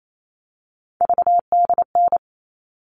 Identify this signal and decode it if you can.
Morse code, heard as a keyed tone, about 28 wpm, 700 Hz. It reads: VDN
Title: All Cut Numbers - 3-Digit 28wpm